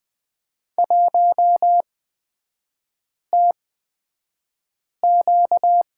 Morse code, heard as a keyed tone, about 20 wpm, spaced as for 5 wpm, 700 Hz. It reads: 1TQ
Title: 236 Lesson 38e - Letter q - Three Characters 20wpm